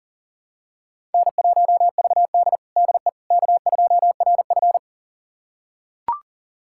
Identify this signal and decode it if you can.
Morse code, heard as a keyed tone, about 40 wpm, 700 Hz. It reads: N1VD DE K2RF
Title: USA Callsign DE Callsign - 4 Characters - 40wpm